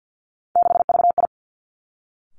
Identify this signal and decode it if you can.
Morse code, heard as a keyed tone, about 50 wpm, 700 Hz. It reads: 6VI
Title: Three Challenging Letter-Number Combinations 50wpm